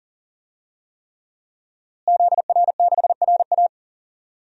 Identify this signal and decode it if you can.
Morse code, heard as a keyed tone, about 40 wpm, 700 Hz. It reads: ZR6RA
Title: DXCC-100 Call Signs - 5 Characters - Repeated 3x - 40wpm